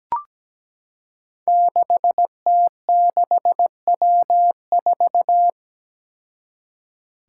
Morse code, heard as a keyed tone, about 17 wpm, 700 Hz. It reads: 6T6W4
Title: Five Challenging Letter-Number Combinations 17wpm